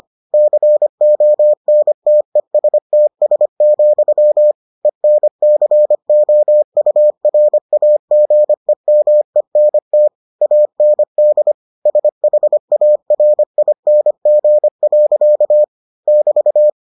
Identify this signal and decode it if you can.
Morse code, heard as a keyed tone, about 25 wpm, 600 Hz. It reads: CONTESTS, ENCOURAGEMENT AND SHARING. <BT>